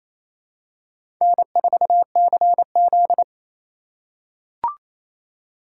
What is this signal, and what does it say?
Morse code, heard as a keyed tone, about 28 wpm, 700 Hz. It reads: N4CZ
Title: USA Call Signs - 4 Characters - 4x Character Spacing - 28wpm